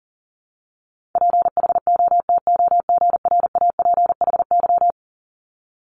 Morse code, heard as a keyed tone, about 41 wpm, 700 Hz. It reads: PHOTOGRAPHY
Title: Multi-letter Phonemes Repeated 3x 40wpm